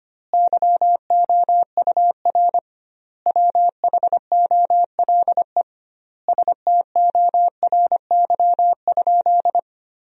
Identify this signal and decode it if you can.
Morse code, heard as a keyed tone, about 25 wpm, 700 Hz. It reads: YOUR WHOLE STORY?
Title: 2x Word-Spacing - Sentences from Top 400 Words - 25wpm